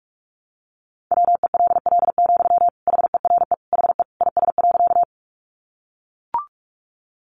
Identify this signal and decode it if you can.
Morse code, heard as a keyed tone, about 44 wpm, 700 Hz. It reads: WELL, HERE HE IS.
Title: Sentences from Top 200 Words 45wpm